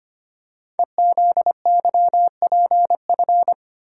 Morse code, heard as a keyed tone, about 25 wpm, 700 Hz. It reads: EZYPF
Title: Code Groups - 5 Letters - 25wpm